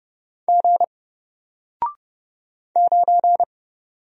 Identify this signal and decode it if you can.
Morse code, heard as a keyed tone, about 30 wpm, 700 Hz. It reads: G 9
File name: Single-Letter-Number-30wpm